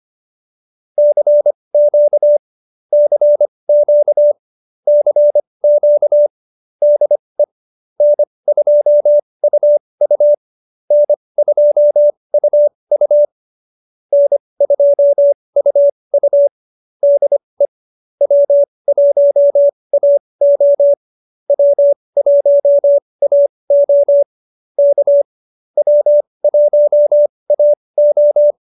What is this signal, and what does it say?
Morse code, heard as a keyed tone, about 25 wpm, spaced as for 20 wpm, 590 Hz. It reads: CQ CQ CQ DE N2UU N2UU N2UU DE W1AO W1AO K W1AO
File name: QSO313 20